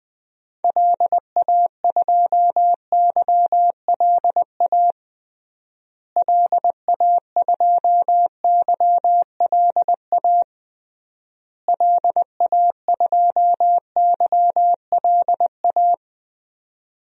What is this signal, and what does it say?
Morse code, heard as a keyed tone, about 20 wpm, 700 Hz. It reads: LA2YLA LA2YLA LA2YLA
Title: DXCC-100 Call Signs - 6 Characters - Repeated 3x - 20wpm